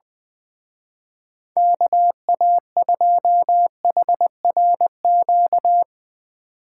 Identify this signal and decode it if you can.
Morse code, heard as a keyed tone, about 19 wpm, 700 Hz. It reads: KA2HRQ
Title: Speed Racing - USA 2x3 Call Signs 17wpm